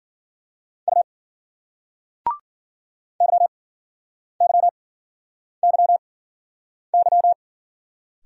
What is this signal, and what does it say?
Morse code, heard as a keyed tone, about 48 wpm, spaced as for 9 wpm, 700 Hz. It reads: U YYYY
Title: Speed-Racing Single Letters 40wpm